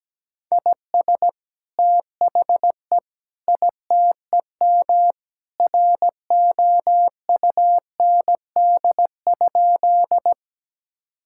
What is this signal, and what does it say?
Morse code, heard as a keyed tone, about 17 wpm, 700 Hz. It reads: IS THE ITEM ROUND?
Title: Sentences from Top 1000 Words - Review Part 6 - 17wpm